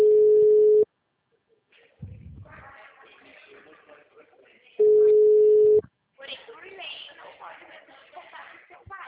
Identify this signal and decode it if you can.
Ringback tone (Europe, 425 Hz), with speech.